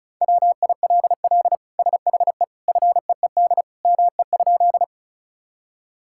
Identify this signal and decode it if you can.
Morse code, heard as a keyed tone, about 36 wpm, 700 Hz. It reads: WILL SHE FEED ME?
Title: Sentences from Top 1600 Words 35wpm